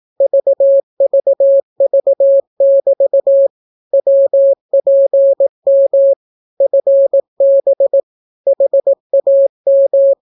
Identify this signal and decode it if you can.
Morse code, heard as a keyed tone, about 18 wpm, 550 Hz. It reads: VVV<BT> WPM FB HAM
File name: naukapodstaw / skroty / skroty